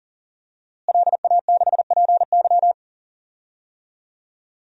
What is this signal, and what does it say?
Morse code, heard as a keyed tone, about 37 wpm, 700 Hz. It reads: LA6PY
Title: DXCC-100 Call Signs - 5 Characters - Repeated 2x - 40wpm